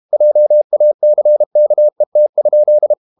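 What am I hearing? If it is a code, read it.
Morse code, heard as a keyed tone, about 32 wpm, 600 Hz. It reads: JACKET?
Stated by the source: Morse code practice at 32 wpm from CW med Gnister, reading Ken Follett's novel Never in English.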